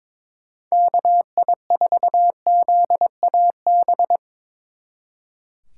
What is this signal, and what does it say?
Morse code, heard as a keyed tone, about 24 wpm, 700 Hz. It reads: KI4ZAB
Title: Speed Racing - USA 2x3 Call Signs 22wpm